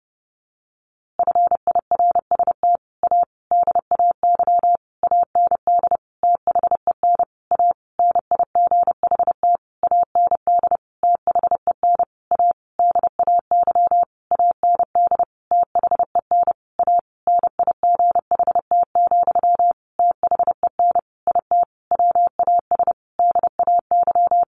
Morse code, heard as a keyed tone, about 30 wpm, 700 Hz. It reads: FIRST A DAY AND THEN A NIGHT AND THEN A DAY AND THEN A NIGHT, THEN IT WAS DAY
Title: A Sound Of Thunder 30wpm